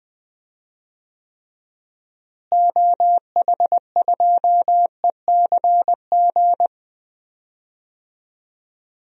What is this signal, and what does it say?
Morse code, heard as a keyed tone, about 21 wpm, 700 Hz. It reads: OH2ECG
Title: DXCC-100 Call Signs - 6 Characters - Repeated 3x - 20wpm